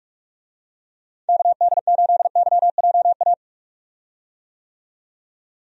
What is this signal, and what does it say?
Morse code, heard as a keyed tone, about 46 wpm, 700 Hz. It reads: KD8YJA
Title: USA Call Signs - 6 Characters - 45wpm